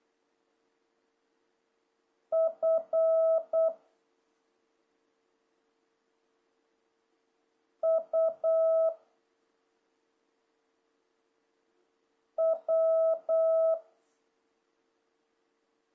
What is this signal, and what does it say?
Morse code, heard as a keyed tone, about 8 wpm, 650 Hz. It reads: F U W
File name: jh8cbh / 2026cw